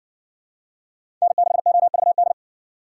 Morse code, heard as a keyed tone, about 55 wpm, 700 Hz. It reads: N6OVD